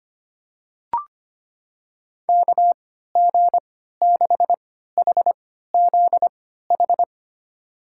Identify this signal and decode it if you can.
Morse code, heard as a keyed tone, about 25 wpm, spaced as for 14 wpm, 700 Hz. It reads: KG6HZH
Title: USA Call Signs - 6 Characters - 3x Character Spacing - 25wpm